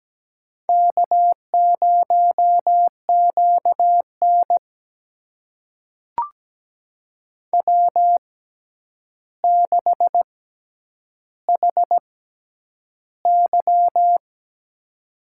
Morse code, heard as a keyed tone, about 17 wpm, spaced as for 9 wpm, 700 Hz. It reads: K0QN W6HY
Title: USA Call Signs - 4 Characters - 6x Character Spacing - 17wpm